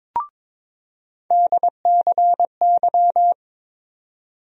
Morse code, heard as a keyed tone, about 22 wpm, 700 Hz. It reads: DCY
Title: USA Call Sign Postfixes - 3 Letters 22wpm